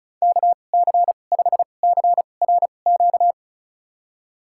Morse code, heard as a keyed tone, about 34 wpm, spaced as for 25 wpm, 700 Hz. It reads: KC5CRQ
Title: USA Call Signs - 6 Characters - 2x Character Spacing - 35wpm